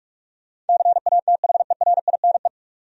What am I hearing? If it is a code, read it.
Morse code, heard as a keyed tone, about 46 wpm, 700 Hz. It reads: CATHERINE
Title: USA Common Names - 45wpm